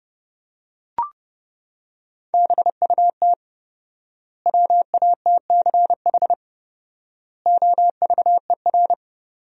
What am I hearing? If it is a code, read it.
Morse code, heard as a keyed tone, about 30 wpm, 700 Hz. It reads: BUT WATCH OVER
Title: 4x Word-Spacing - Sentences from Top 300 Words - 30wpm